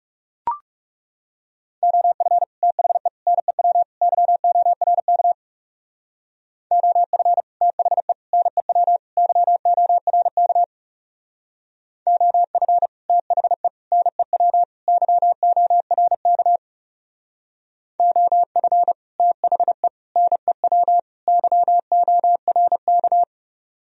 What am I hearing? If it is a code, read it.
Morse code, heard as a keyed tone, about 36 wpm, 700 Hz. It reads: OF THE NEW YORK OF THE NEW YORK OF THE NEW YORK OF THE NEW YORK
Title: Speed-Racing - Sets of 4 Words 30wpm